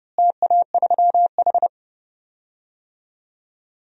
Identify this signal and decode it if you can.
Morse code, heard as a keyed tone, about 30 wpm, 700 Hz. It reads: TA3H
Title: DXCC-100 Call Signs - 4 Characters - Repeated 2x - 30wpm